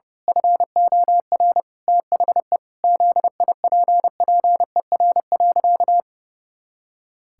Morse code, heard as a keyed tone, about 32 wpm, 700 Hz. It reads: FOR THE ZIPPER.
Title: Pangram Challenge v2 - 30wpm